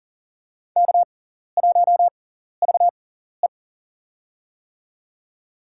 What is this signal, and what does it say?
Morse code, heard as a keyed tone, about 40 wpm, spaced as for 14 wpm, 700 Hz. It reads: K1VE
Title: USA Call Signs - 4 Characters - 6x Character Spacing - 40wpm